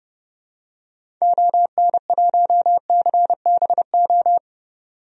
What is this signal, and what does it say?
Morse code, heard as a keyed tone, about 30 wpm, 700 Hz. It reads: ON1CBO
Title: DXCC-100 Call Signs - 6 Characters - 30wpm